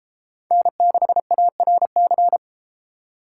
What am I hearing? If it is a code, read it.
Morse code, heard as a keyed tone, about 29 wpm, 700 Hz. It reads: N6ARC